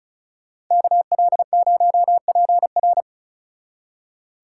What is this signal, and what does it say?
Morse code, heard as a keyed tone, about 40 wpm, 700 Hz. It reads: KL0PR